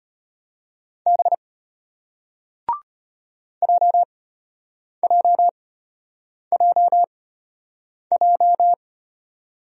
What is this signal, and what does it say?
Morse code, heard as a keyed tone, about 34 wpm, spaced as for 8 wpm, 700 Hz. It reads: B JJJJ